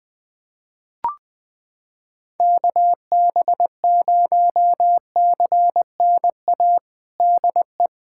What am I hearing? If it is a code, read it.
Morse code, heard as a keyed tone, about 20 wpm, 700 Hz. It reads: KB0CNA DE
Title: USA Callsign DE Callsign - 6 Characters - 20wpm